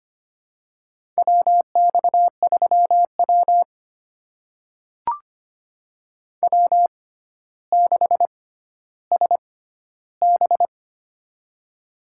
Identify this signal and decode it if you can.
Morse code, heard as a keyed tone, about 25 wpm, spaced as for 13 wpm, 700 Hz. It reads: WX3W W6SB
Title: USA Call Signs - 4 Characters - 6x Character Spacing - 25wpm